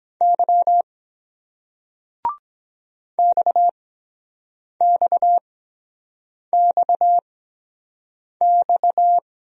Morse code, heard as a keyed tone, about 23 wpm, spaced as for 7 wpm, 700 Hz. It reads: Y XXXX